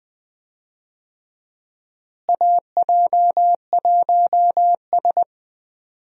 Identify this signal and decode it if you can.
Morse code, heard as a keyed tone, about 21 wpm, 700 Hz. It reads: AJ1S
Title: USA Call Signs - 4 Characters - Repeated 2x - 20wpm